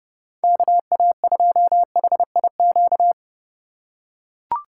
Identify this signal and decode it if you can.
Morse code, heard as a keyed tone, about 29 wpm, 700 Hz. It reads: KA2HIQ
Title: USA Call Signs - 6 Characters - 3x Character Spacing - 30wpm